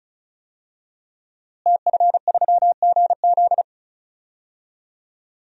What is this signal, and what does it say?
Morse code, heard as a keyed tone, about 34 wpm, 700 Hz. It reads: TF3GZ
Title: DXCC-100 Call Signs - 5 Characters - Repeated 3x - 35wpm